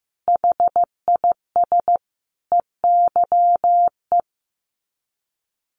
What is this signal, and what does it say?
Morse code, heard as a keyed tone, about 15 wpm, 700 Hz. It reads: HIS EYE
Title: Sets of 5 Words Encore2 15wpm